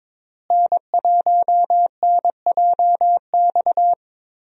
Morse code, heard as a keyed tone, about 22 wpm, 700 Hz. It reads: N1NJX
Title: USA Call Signs - 5 Characters - Repeated 2x - 22wpm